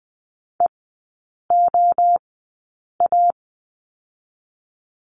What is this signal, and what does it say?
Morse code, heard as a keyed tone, about 20 wpm, 700 Hz. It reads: E O A
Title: 023 Lesson 5e - Letter o - Three Characters 20wpm